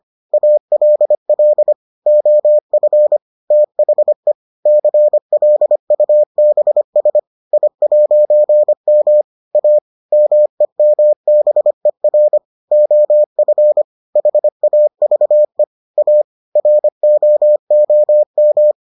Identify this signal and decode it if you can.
Morse code, heard as a keyed tone, about 25 wpm, 600 Hz. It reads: ALL OF THE CLUBS I'M A MEMBER OF HAVE A ROOM